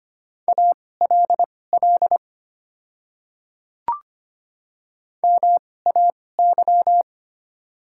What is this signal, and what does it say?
Morse code, heard as a keyed tone, about 25 wpm, spaced as for 18 wpm, 690 Hz. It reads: ALL MAY
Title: 2x Character Spacing - Top 300 Words - 25wpm